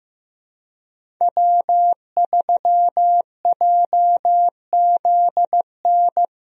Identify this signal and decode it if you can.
Morse code, heard as a keyed tone, about 16 wpm, 700 Hz. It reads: W3JZN
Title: USA Call Signs - 5 Characters - Repeated 2x - 15wpm